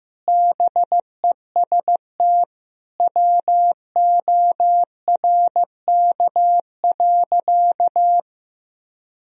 Morse code, heard as a keyed tone, about 15 wpm, 700 Hz. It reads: BEST WORK.
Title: Sentences from Top 1000 Words - Review Part 3 - 15wpm